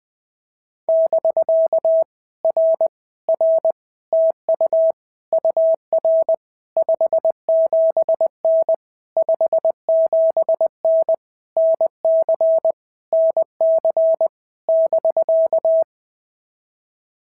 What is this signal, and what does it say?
Morse code, heard as a keyed tone, about 20 wpm, 650 Hz. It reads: <BK> R R TU UR 57N 57N NC NC <BK>
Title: POTA QSO Practice - 20wpm